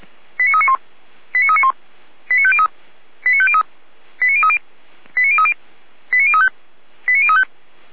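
Five-tone selective call (ZVEI), heard as quick runs of five tones, tones of about 70 ms. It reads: 89391 89391 89503 89503 89030 89030 89035 89035
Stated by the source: ZVEI five-tone selective call received on SDR.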